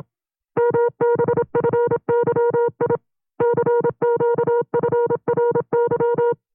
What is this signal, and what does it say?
Morse code, heard as a keyed tone, about 26 wpm, 470 Hz. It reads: MBFYI CQFRY
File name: FreeFiles / CW-practice / 25WPM-side2